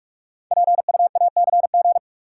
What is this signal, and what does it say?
Morse code, heard as a keyed tone, about 44 wpm, 700 Hz. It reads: PUACG
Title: Code Groups - 5 Letters - 45wpm